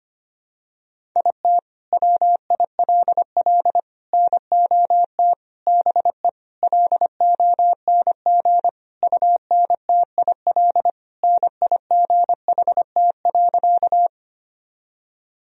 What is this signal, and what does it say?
Morse code, heard as a keyed tone, about 25 wpm, 700 Hz. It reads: IT WILL NOT BE LONG UNTIL NIGHT.
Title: Sentences from Top 1000 Words - Review Part 1 - 25wpm